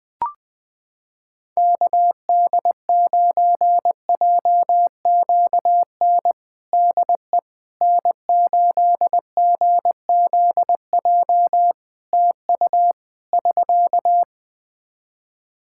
Morse code, heard as a keyed tone, about 20 wpm, 700 Hz. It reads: KD9JQN DE N8GZJ TU <SK>
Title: K1USN SST QSO Practice 20wpm